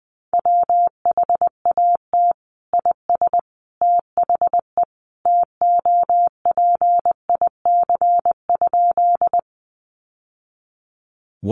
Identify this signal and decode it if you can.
Morse code, heard as a keyed tone, about 20 wpm, 700 Hz. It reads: WHAT IS THE TOPIC?
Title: Sentences from Top 1200 Words 20wpm